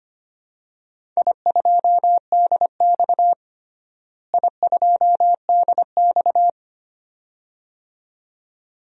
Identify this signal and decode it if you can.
Morse code, heard as a keyed tone, about 25 wpm, 700 Hz. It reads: I2DX I2DX
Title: DXCC-100 Call Signs - 4 Characters - Repeated 2x - 25wpm